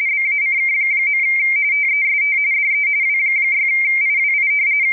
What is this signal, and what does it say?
RTTY, about 45.45 baud. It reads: BACK. TEST DE W6IWI
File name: rtty / websdr recording 14086.0kHz